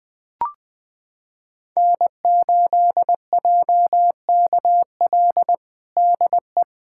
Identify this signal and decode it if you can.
Morse code, heard as a keyed tone, about 20 wpm, 700 Hz. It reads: N8JKL DE
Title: USA Callsign DE Callsign - 5 Characters - 20wpm